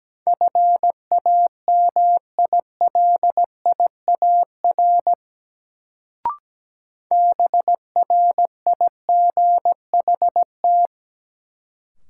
Morse code, heard as a keyed tone, about 17 wpm, 700 Hz. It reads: FAMILIAR BRIGHT
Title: Sentences from Top 1000 Words - New Word Drill - 17wpm